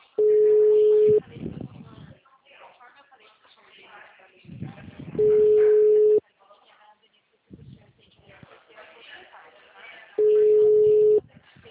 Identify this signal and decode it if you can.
Ringback tone (Europe, 425 Hz), with speech.
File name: out- -2015- - - .883